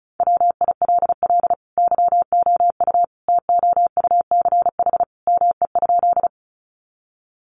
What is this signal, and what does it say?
Morse code, heard as a keyed tone, about 34 wpm, 700 Hz. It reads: WILL YOU TOUCH ME?
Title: Sentences from Top 1200 Words 35wpm